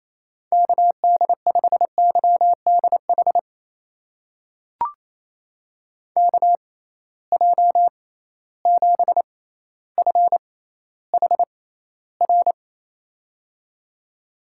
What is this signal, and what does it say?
Morse code, heard as a keyed tone, about 28 wpm, spaced as for 14 wpm, 700 Hz. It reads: KD5YDH KJ7FHR
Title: USA Call Signs - 6 Characters - 6x Character Spacing - 28wpm